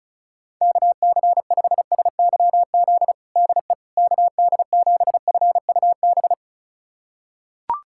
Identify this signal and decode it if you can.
Morse code, heard as a keyed tone, about 36 wpm, 700 Hz. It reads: KC5SYZ DE KD7FUB
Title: USA Callsign DE Callsign - 6 Characters - 35wpm